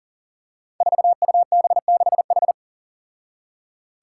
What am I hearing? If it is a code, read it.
Morse code, heard as a keyed tone, about 40 wpm, 700 Hz. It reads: 4UB6H
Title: Five Challenging Letter-Number Combinations 40wpm